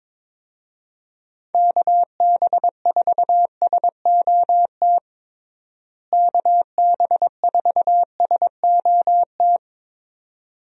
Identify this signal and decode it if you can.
Morse code, heard as a keyed tone, about 22 wpm, 700 Hz. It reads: KB4SOT KB4SOT
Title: USA Call Signs - 6 Characters - Repeated 2x - 22wpm